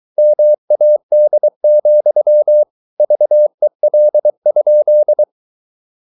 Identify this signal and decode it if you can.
Morse code, heard as a keyed tone, about 23 wpm, 600 Hz. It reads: MAD, VEL?
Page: Ferie 23 wpm